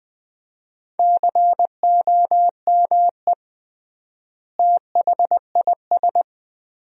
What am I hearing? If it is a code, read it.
Morse code, heard as a keyed tone, about 20 wpm, 700 Hz. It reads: COME THIS
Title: 3x Word-Spacing - Sentences from Top 100 Words - 20wpm